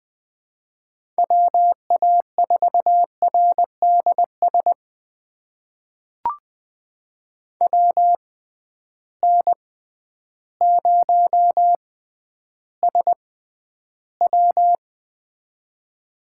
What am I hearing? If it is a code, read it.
Morse code, heard as a keyed tone, about 20 wpm, spaced as for 11 wpm, 700 Hz. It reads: WA4RDS WN0SW
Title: USA Call Signs - 6x Character Spacing - 20wpm